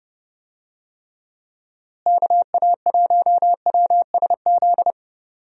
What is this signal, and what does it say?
Morse code, heard as a keyed tone, about 30 wpm, 700 Hz. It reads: KA1WSZ
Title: USA Call Signs - 6 Characters - Repeated 2x - 30wpm